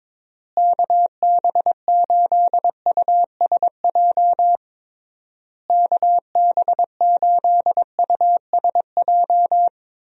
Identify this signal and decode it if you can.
Morse code, heard as a keyed tone, about 22 wpm, 700 Hz. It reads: KB8USJ KB8USJ
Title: USA Call Signs - 6 Characters - Repeated 3x - 22wpm